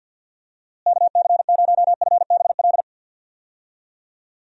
Morse code, heard as a keyed tone, about 45 wpm, 700 Hz. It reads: KC0FBL
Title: USA Call Signs - 6 Characters - Repeated 2x - 50wpm